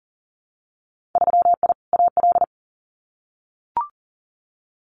Morse code, heard as a keyed tone, about 40 wpm, 700 Hz. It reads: 3I AL
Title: Winter Field Day Exchanges Encore 40wpm